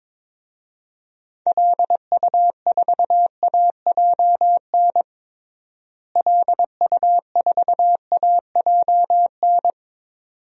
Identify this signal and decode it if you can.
Morse code, heard as a keyed tone, about 21 wpm, 700 Hz. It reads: LU4AJN LU4AJN
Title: DXCC-100 Call Signs - 6 Characters - Repeated 2x - 22wpm